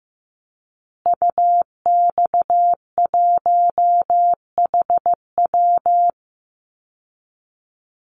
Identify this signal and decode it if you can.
Morse code, heard as a keyed tone, about 15 wpm, 700 Hz. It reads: UX1HW
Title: Call Signs Repeated 2x - 15wpm